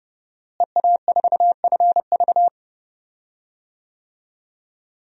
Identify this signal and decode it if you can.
Morse code, heard as a keyed tone, about 30 wpm, 700 Hz. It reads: EA4FV
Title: DXCC-100 Call Signs - 5 Characters - Repeated 2x - 30wpm